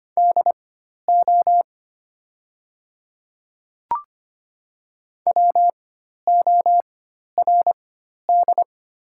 Morse code, heard as a keyed tone, about 25 wpm, spaced as for 12 wpm, 690 Hz. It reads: DO WORD
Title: 4x Character Spacing - Top 100 Words - 25wpm